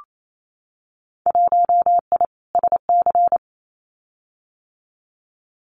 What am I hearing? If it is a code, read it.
Morse code, heard as a keyed tone, about 28 wpm, 700 Hz. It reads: 1I SC
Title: Winter Field Day Exchanges Encore 28wpm